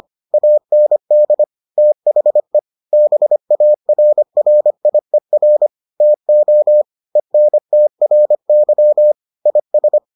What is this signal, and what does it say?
Morse code, heard as a keyed tone, about 25 wpm, 600 Hz. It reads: AND THE BARRIER TO ENTRY IS